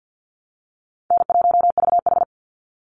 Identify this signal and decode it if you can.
Morse code, heard as a keyed tone, about 50 wpm, 700 Hz. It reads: N1VH
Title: Easy Call Signs - 50wpm